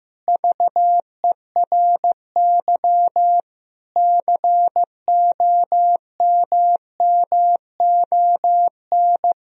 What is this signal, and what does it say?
Morse code, heard as a keyed tone, about 15 wpm, 700 Hz. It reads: VERY COMMON
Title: Sentences from Top 1000 Words - Review Part 4 - 15wpm